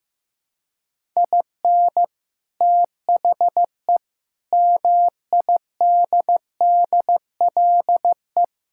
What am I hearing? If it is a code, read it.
Morse code, heard as a keyed tone, about 15 wpm, 700 Hz. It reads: IN THE MIDDLE
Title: Sentences from Top 1000 Words - Review Part 6 - 15wpm